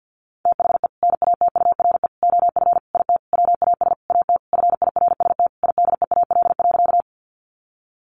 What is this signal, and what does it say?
Morse code, heard as a keyed tone, about 50 wpm, 700 Hz. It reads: THE NATURE OF IT WAS AT FIRST IDEAL.
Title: Sentences from Top 1400 Words 50wpm